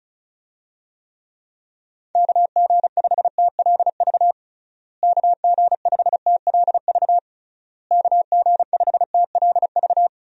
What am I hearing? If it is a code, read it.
Morse code, heard as a keyed tone, about 36 wpm, 700 Hz. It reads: KG5TLV KG5TLV KG5TLV
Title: USA Call Signs - 6 Characters - Repeated 3x - 35wpm